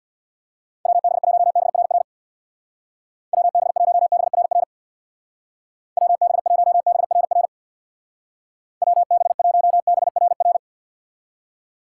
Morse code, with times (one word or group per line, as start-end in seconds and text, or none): 0.9-2.0 WB1BRR
3.3-4.6 WB1BRR
6.0-7.5 WB1BRR
8.8-10.6 WB1BRR